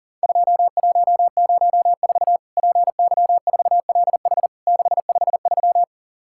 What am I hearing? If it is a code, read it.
Morse code, heard as a keyed tone, about 38 wpm, 700 Hz. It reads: 2104 PY4LH 653
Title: Worked All Europe DX Contest - QTC Exchange - 40wpm